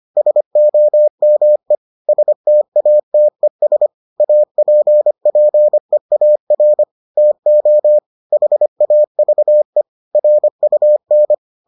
Morse code, heard as a keyed tone, about 25 wpm, 600 Hz. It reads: SOME STATES APPEAR TO HAVE RUN